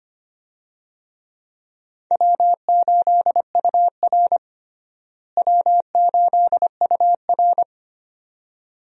Morse code, heard as a keyed tone, about 25 wpm, 700 Hz. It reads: W8UR W8UR
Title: USA Call Signs - 4 Characters - Repeated 2x - 25wpm